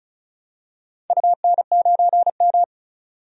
Morse code, heard as a keyed tone, about 36 wpm, 700 Hz. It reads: UN9M
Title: DXCC-100 Call Signs - 4 Characters - 35wpm